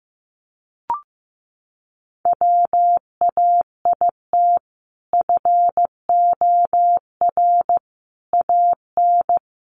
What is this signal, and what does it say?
Morse code, heard as a keyed tone, about 15 wpm, 700 Hz. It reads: WAIT FOR AN
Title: Sentences from Top 1000 Words 15wpm